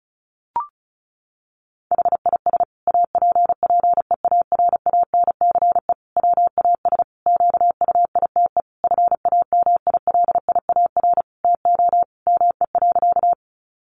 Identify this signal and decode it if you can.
Morse code, heard as a keyed tone, about 34 wpm, 700 Hz. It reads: HIS APPEARANCE WAS QUITE FAMILIAR TO ME.
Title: Sentences from Top 1100 Words 35wpm